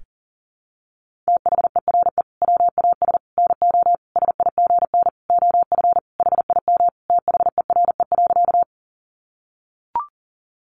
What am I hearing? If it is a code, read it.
Morse code, heard as a keyed tone, about 40 wpm, 700 Hz. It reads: THERE WAS NO SIGN OF HIM THERE.
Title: Sentences from Top 1000 Words 40wpm